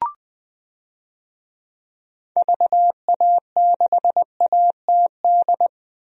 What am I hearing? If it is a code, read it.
Morse code, heard as a keyed tone, about 20 wpm, 700 Hz. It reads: VA6ATD
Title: DXCC-100 Call Signs - 6 Characters - 20wpm